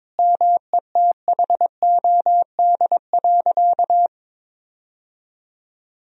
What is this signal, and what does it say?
Morse code, heard as a keyed tone, about 22 wpm, 700 Hz. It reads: METHOD.
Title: Sentences from Top 1000 Words - Review Part 3 - 22wpm